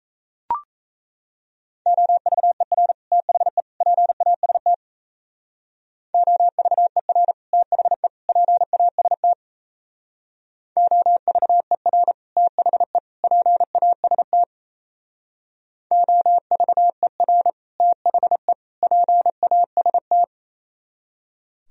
Morse code, heard as a keyed tone, about 37 wpm, 700 Hz. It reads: OVER THE PAST OVER THE PAST OVER THE PAST OVER THE PAST